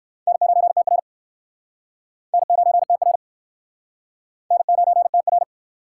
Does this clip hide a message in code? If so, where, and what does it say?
Yes: Morse code from 0.3 to 5.4 s, heard as a keyed tone, about 61 wpm, 700 Hz. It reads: N9TR N9TR N9TR